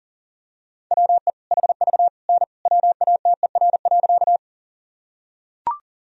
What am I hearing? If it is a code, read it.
Morse code, heard as a keyed tone, about 40 wpm, 700 Hz. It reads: WE HV N WATER.
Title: Sentences from Top 100 Words and Abbreviations 40wpm